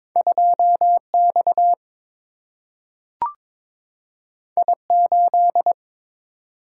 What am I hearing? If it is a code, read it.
Morse code, heard as a keyed tone, about 22 wpm, 700 Hz. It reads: 2 X I 8
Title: Single-Letter Single-Number 22wpm